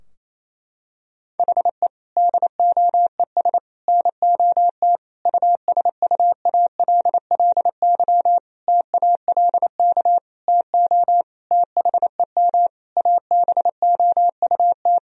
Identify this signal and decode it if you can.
Morse code, heard as a keyed tone, about 28 wpm, 700 Hz. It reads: HE DOES NOT USUALLY TALK TO THEM ABOUT
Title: Sentences from Top 1000 Words - Review Part 3 - 28wpm